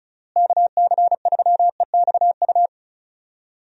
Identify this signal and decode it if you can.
Morse code, heard as a keyed tone, about 36 wpm, 700 Hz. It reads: KC3EXU
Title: USA Call Signs - 6 Characters - Repeated 2x - 35wpm